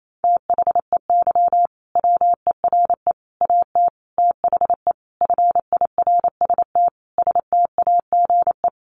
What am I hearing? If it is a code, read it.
Morse code, heard as a keyed tone, about 28 wpm, 700 Hz. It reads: THEY WERE AT THE FIRST STAGE
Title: Sentences from Top 1200 Words 28wpm